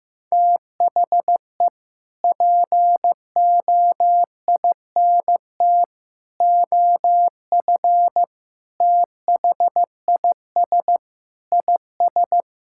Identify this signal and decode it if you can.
Morse code, heard as a keyed tone, about 15 wpm, 700 Hz. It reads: THE POINT OF THIS IS
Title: Sentences from Top 1600 Words 15wpm